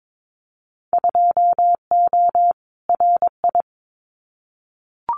Morse code, heard as a keyed tone, about 22 wpm, 700 Hz. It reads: 2O RI
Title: Winter Field Day Exchanges 22wpm